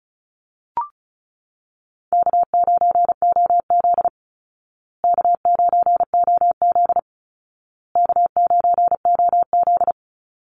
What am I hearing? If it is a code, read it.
Morse code, heard as a keyed tone, about 34 wpm, 700 Hz. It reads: K9OZ K9OZ K9OZ
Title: Easy Call Signs Repeated 3x - 35wpm